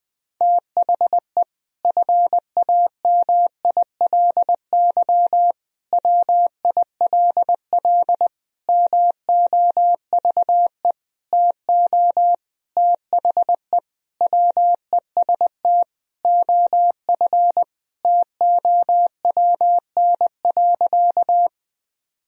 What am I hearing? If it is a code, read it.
Morse code, heard as a keyed tone, about 21 wpm, 700 Hz. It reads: THE FAMILY WILL MOVE TO THE WEST OF TOWN.
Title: Sentences from Top 1000 Words - Review Part 2 - 20wpm